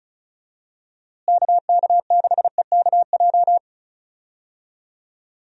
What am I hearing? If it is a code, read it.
Morse code, heard as a keyed tone, about 36 wpm, 700 Hz. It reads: KK6EKJ
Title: USA Call Signs - 6 Characters - Repeated 3x - 35wpm